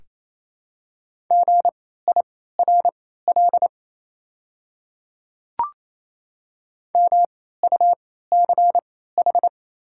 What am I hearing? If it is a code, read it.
Morse code, heard as a keyed tone, about 28 wpm, spaced as for 16 wpm, 690 Hz. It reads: GIRL MUCH
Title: 3x Character Spacing - Top 500 Words - 28wpm